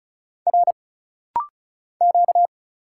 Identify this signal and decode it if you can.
Morse code, heard as a keyed tone, about 34 wpm, 700 Hz. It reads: R Q
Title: Single Letters - Rapid-Fire 35wpm